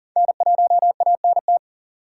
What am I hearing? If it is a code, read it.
Morse code, heard as a keyed tone, about 40 wpm, 700 Hz. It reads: N1ANT
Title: USA Call Signs - 5 Characters - 3x Character Spacing - 40wpm